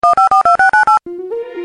DTMF, heard as telephone touch tones, tones of about 130 ms. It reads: 1542698